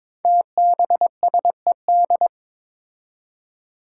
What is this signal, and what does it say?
Morse code, heard as a keyed tone, about 22 wpm, 700 Hz. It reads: TBSED
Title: Five Challenging Letter-Number Combinations 22wpm